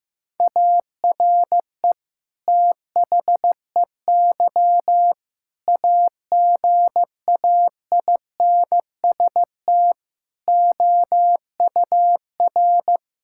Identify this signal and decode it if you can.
Morse code, heard as a keyed tone, about 16 wpm, 700 Hz. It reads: ARE THEY AGAINST OUR
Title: Sentences from Top 1000 Words - Review Part 5 - 15wpm